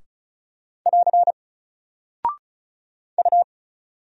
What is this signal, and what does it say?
Morse code, heard as a keyed tone, about 34 wpm, 700 Hz. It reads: <AR> U
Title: Single Character (Letter:Number:Punctuation:Prosign) 35wpm